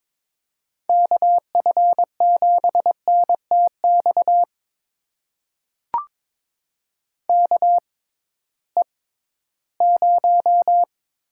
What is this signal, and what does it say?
Morse code, heard as a keyed tone, about 22 wpm, spaced as for 14 wpm, 700 Hz. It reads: KF7NTX KE0
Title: USA Call Signs - 6 Characters - 6x Character Spacing - 22wpm